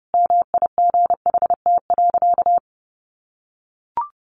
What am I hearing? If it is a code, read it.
Morse code, heard as a keyed tone, about 30 wpm, 700 Hz. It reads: MIGHT.
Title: Sentences from Top 500 Words 30wpm